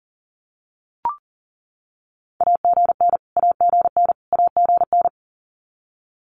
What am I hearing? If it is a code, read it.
Morse code, heard as a keyed tone, about 40 wpm, 700 Hz. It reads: AGN AGN AGN
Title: Abbreviations Repeated 3x - 40wpm